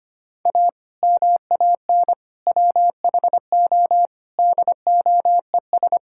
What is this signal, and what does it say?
Morse code, heard as a keyed tone, about 25 wpm, 700 Hz. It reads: A MAN WHO DOES
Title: Sentences from Top 1600 Words 25wpm